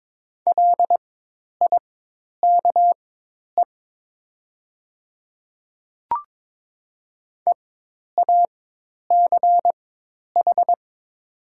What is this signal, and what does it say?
Morse code, heard as a keyed tone, about 22 wpm, spaced as for 10 wpm, 700 Hz. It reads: LIKE EACH
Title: 4x Character Spacing - Top 100 Words - 22wpm